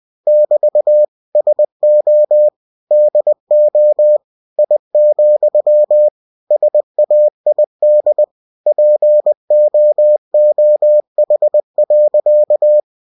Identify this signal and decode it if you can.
Morse code, heard as a keyed tone, about 20 wpm, 600 Hz. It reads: <BT> SO DO I, SAID POOH.
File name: cw-books / winnie-the-pooh / winnie-20wpm-page-0044